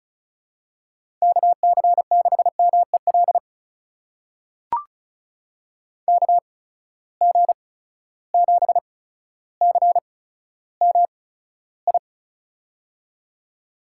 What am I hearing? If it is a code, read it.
Morse code, heard as a keyed tone, about 36 wpm, spaced as for 15 wpm, 700 Hz. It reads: KC6MEL KG7CMI
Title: USA Call Signs - 6 Characters - 8x Character Spacing - 35wpm